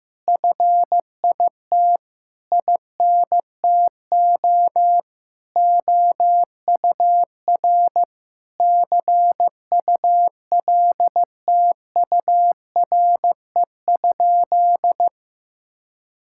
Morse code, heard as a keyed tone, about 15 wpm, 700 Hz. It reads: FIT INTO OUR CULTURE?
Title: Sentences from Top 1000 Words - Review Part 6 - 15wpm